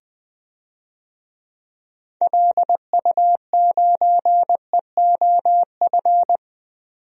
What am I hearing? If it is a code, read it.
Morse code, heard as a keyed tone, about 21 wpm, 700 Hz. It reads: LU9EOF
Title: DXCC-100 Call Signs - 6 Characters - 20wpm